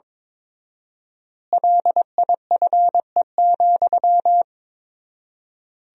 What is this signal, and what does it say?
Morse code, heard as a keyed tone, about 22 wpm, 700 Hz. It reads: LIFE,
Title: 4x Word-Spacing - Sentences from Top 300 Words - 22wpm